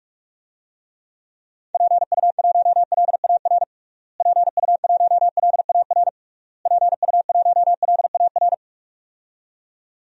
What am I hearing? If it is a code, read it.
Morse code, heard as a keyed tone, about 46 wpm, 700 Hz. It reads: PU1LAR PU1LAR PU1LAR
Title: DXCC-100 Call Signs - 6 Characters - Repeated 3x - 45wpm